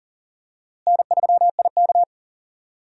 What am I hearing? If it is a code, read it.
Morse code, heard as a keyed tone, about 40 wpm, 700 Hz. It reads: N3IK
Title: USA Call Signs - 4 Characters - Repeated 3x - 40wpm